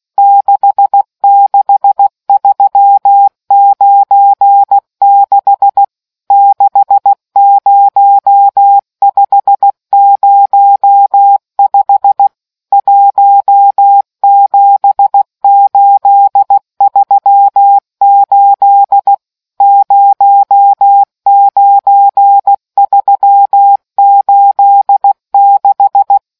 Morse code, heard as a keyed tone, about 16 wpm, 800 Hz. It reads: 66396 60505 17838 09386